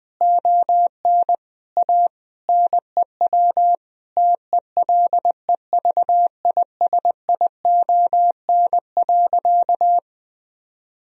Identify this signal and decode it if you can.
Morse code, heard as a keyed tone, about 21 wpm, 700 Hz. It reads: ON A NEW TELEVISION.
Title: Sentences from Top 1000 Words - Review Part 4 - 20wpm